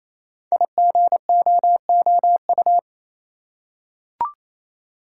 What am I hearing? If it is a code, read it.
Morse code, heard as a keyed tone, about 29 wpm, 700 Hz. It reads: IGOOU
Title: Code Groups - 5 Letters - 28wpm